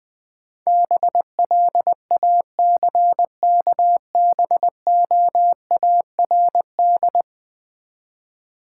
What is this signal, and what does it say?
Morse code, heard as a keyed tone, about 21 wpm, 700 Hz. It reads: BLACKBOARD
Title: Easy Words with Hints - 20wpm